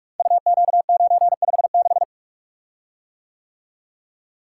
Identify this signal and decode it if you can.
Morse code, heard as a keyed tone, about 44 wpm, 690 Hz. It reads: UQ956
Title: Code Groups - 5 Letter:Numbers - 45wpm